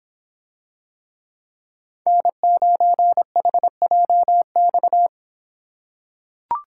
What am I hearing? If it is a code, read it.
Morse code, heard as a keyed tone, about 26 wpm, 700 Hz. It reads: N9HJX